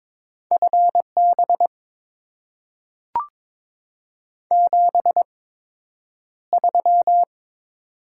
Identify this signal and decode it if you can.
Morse code, heard as a keyed tone, about 22 wpm, spaced as for 9 wpm, 700 Hz. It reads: FB 73
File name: Top 20 QSO Elements - 8x Character Spacing - 22wpm